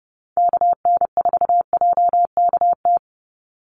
Morse code, heard as a keyed tone, about 30 wpm, 700 Hz. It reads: KN4JKT
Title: USA Call Signs 30wpm